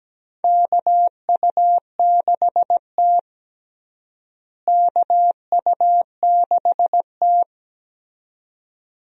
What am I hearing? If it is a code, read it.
Morse code, heard as a keyed tone, about 17 wpm, 700 Hz. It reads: KU6T KU6T
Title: USA Call Signs - 4 Characters - Repeated 2x - 17wpm